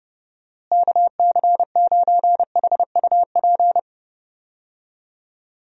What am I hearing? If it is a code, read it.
Morse code, heard as a keyed tone, about 32 wpm, 700 Hz. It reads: KC9HUP
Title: USA Call Signs - 6 Characters - Repeated 3x - 30wpm